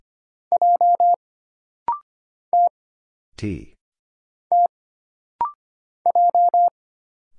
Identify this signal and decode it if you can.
Morse code, heard as a keyed tone, about 25 wpm, 700 Hz. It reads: J T T J
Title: Single Letter-Number Alt - Rapid-Fire 25wpm